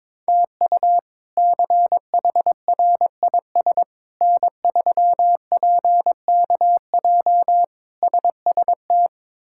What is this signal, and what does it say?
Morse code, heard as a keyed tone, about 22 wpm, 700 Hz. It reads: TU CHRIS N3PKJ SST
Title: K1USN SST QSO Practice - 22wpm